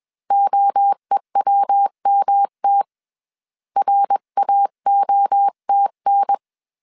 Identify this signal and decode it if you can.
Morse code, heard as a keyed tone, about 21 wpm, 800 Hz. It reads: OEWMT RAOTN
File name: CW / DET / 1 3 1500C S B 20wpm 1754